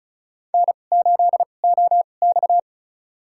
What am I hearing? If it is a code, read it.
Morse code, heard as a keyed tone, about 36 wpm, spaced as for 25 wpm, 700 Hz. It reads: N8OX
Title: USA Call Signs - 4 Characters - 2x Character Spacing - 35wpm